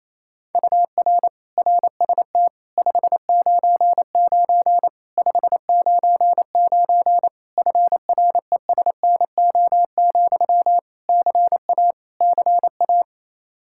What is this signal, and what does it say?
Morse code, heard as a keyed tone, about 28 wpm, 700 Hz. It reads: UR RST 599 599 FRESNO, CA CA
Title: Practice QSO - Exchange 1 - Beginning - 28wpm